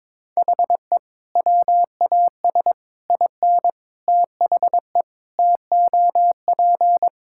Morse code, heard as a keyed tone, about 21 wpm, 700 Hz. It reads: HE WAS IN THE TOP
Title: Sentences from Top 1600 Words 22wpm